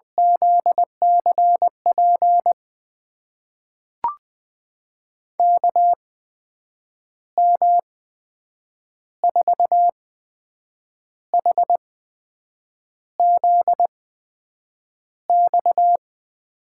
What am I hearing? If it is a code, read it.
Morse code, heard as a keyed tone, about 20 wpm, spaced as for 7 wpm, 700 Hz. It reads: ZCP KM4HZX
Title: USA Call Signs - 6 Characters - 8x Character Spacing - 20wpm